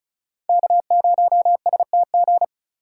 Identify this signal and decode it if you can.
Morse code, heard as a keyed tone, about 34 wpm, 700 Hz. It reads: K0STG
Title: USA Call Signs - 5 Characters - 3x Character Spacing - 35wpm